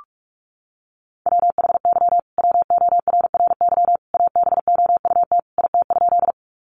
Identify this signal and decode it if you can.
Morse code, heard as a keyed tone, about 44 wpm, 700 Hz. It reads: WHY WORRY ABOUT IT?
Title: Sentences from Top 1100 Words 45wpm